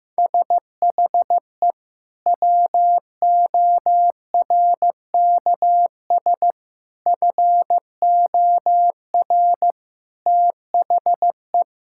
Morse code, heard as a keyed tone, about 16 wpm, 700 Hz. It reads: SHE WORKS FOR THE
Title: Sentences from Top 1000 Words - Review Part 5 - 15wpm